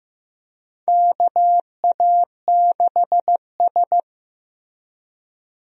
Morse code, heard as a keyed tone, about 15 wpm, 700 Hz. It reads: KA6S
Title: USA Call Signs - 4 Characters - 15wpm